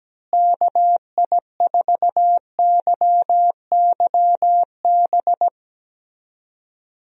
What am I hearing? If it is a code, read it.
Morse code, heard as a keyed tone, about 17 wpm, 700 Hz. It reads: KI4YYB
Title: USA Call Signs - 6 Characters - Repeated 3x - 17wpm